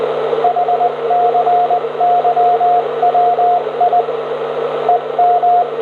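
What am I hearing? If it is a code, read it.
Morse code, heard as a keyed tone, about 20 wpm, 710 Hz, with radio noise. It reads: HCYWI EM
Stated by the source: 160m signal on 1854 kHz, 100W into a 60m antenna, received on an R-311 with an indoor antenna and microphone, 400Hz filter, noise S5.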